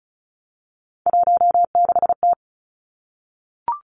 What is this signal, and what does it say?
Morse code, heard as a keyed tone, about 36 wpm, 700 Hz. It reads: 16T
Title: Three Challenging Letter-Number Combinations 35wpm